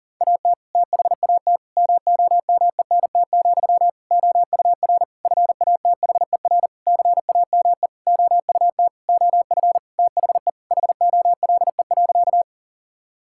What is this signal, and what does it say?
Morse code, heard as a keyed tone, about 37 wpm, 700 Hz. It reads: AT THAT MOMENT, OUR FATHER CAME OUT OF THE HOLE.
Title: Sentences from Top 1600 Words 40wpm